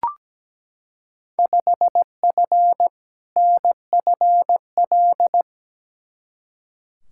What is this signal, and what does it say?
Morse code, heard as a keyed tone, about 17 wpm, 700 Hz. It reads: 5F NFL
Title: Field Day Exchanges Encore - 17wpm